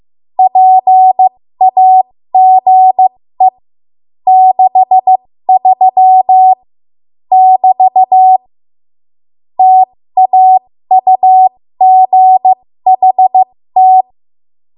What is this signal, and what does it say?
Morse code, heard as a keyed tone, about 15 wpm, spaced as for 13 wpm, 750 Hz. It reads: PAGE 63 <BT> TAUGHT
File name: Morse / 13WPM